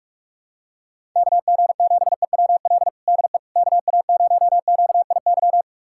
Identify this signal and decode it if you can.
Morse code, heard as a keyed tone, about 46 wpm, 690 Hz. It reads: KG7EWL DE KA0QIY
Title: USA Callsign DE Callsign - 6 Characters - 45wpm